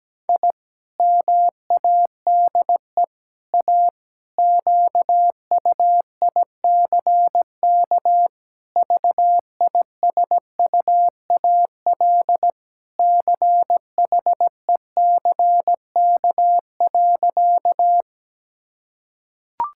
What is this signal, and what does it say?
Morse code, heard as a keyed tone, about 17 wpm, 700 Hz. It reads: I MADE A QUICK VISUAL CHECK.
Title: Sentences from Top 1000 Words - Review Part 6 - 17wpm